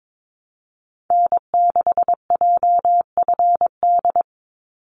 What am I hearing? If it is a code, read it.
Morse code, heard as a keyed tone, about 22 wpm, 700 Hz. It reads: N6JFD
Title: NA SOTA Activator Callsigns 22wpm